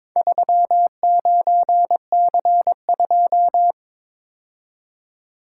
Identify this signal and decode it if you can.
Morse code, heard as a keyed tone, about 22 wpm, 700 Hz. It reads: 39C2
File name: practice / 22wpm / Four Letter-Number Combinations - 22wpm